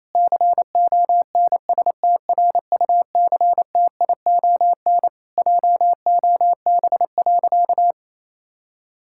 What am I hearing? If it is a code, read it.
Morse code, heard as a keyed tone, about 28 wpm, 700 Hz. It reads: CONSTRUCTION JOB.
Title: Sentences from Top 1600 Words 28wpm